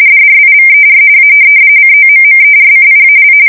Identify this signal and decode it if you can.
RTTY, about 45.45 baud. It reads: DRIVER A COMMERCIAL
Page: 170 Shift 2125/2295